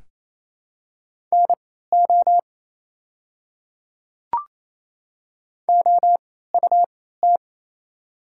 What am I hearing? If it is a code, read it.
Morse code, heard as a keyed tone, about 28 wpm, spaced as for 16 wpm, 700 Hz. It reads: NO OUT
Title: 3x Character Spacing - Top 200 Words - 28wpm